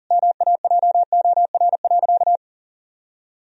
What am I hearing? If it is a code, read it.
Morse code, heard as a keyed tone, about 40 wpm, 700 Hz. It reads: MAJOR.
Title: Sentences from Top 1000 Words - Review Part 4 - 40wpm